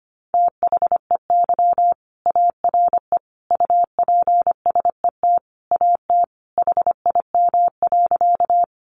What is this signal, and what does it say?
Morse code, heard as a keyed tone, about 25 wpm, 700 Hz. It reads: THEY ARE UPSET AT HIM.
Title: Sentences from Top 800 Words 25wpm